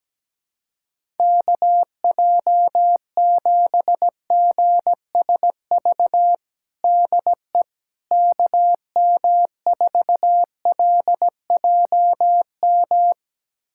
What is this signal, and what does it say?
Morse code, heard as a keyed tone, about 17 wpm, 700 Hz. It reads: KJ7GSV DE KM4LJM
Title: USA Callsign DE Callsign - 6 Characters - 17wpm